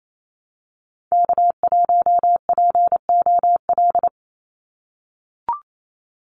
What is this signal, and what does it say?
Morse code, heard as a keyed tone, about 28 wpm, 700 Hz. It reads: K1POL
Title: USA Call Signs 28wpm